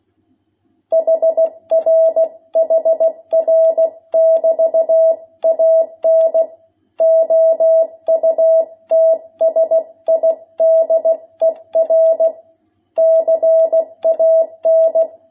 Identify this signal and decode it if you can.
Morse code, heard as a keyed tone, about 16 wpm, 660 Hz. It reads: HR HR <BT> AN OUTSIDER CAN
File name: jh8cbh / 2021cw